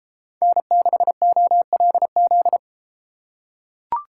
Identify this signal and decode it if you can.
Morse code, heard as a keyed tone, about 30 wpm, 700 Hz. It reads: N6OLZ